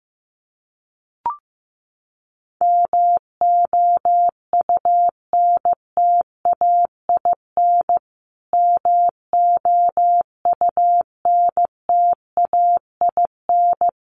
Morse code, heard as a keyed tone, about 15 wpm, 700 Hz. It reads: MOUNTAIN MOUNTAIN
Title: Top 500 Words Repeated 15wpm